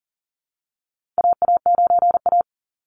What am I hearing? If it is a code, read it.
Morse code, heard as a keyed tone, about 40 wpm, 700 Hz. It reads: AA9A
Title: Call Signs - 40wpm